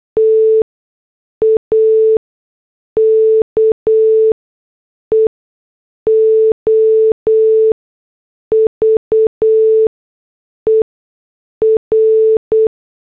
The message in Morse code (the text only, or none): TAKEOVER